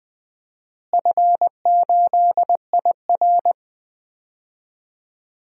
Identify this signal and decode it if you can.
Morse code, heard as a keyed tone, about 20 wpm, 700 Hz. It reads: F8IR
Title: DXCC-100 Call Signs - 4 Characters - Repeated 3x - 20wpm